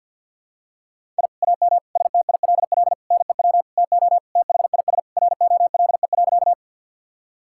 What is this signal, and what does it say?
Morse code, heard as a keyed tone, about 50 wpm, 700 Hz. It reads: I AM STILL NEW TO THIS ROLE.
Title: Sentences from Top 1000 Words - Review Part 6 - 50wpm